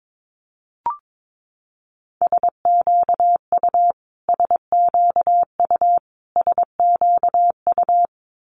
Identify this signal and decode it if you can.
Morse code, heard as a keyed tone, about 22 wpm, 700 Hz. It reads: SQU SQU SQU
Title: Multi-letter Phonemes Repeated 3x 22wpm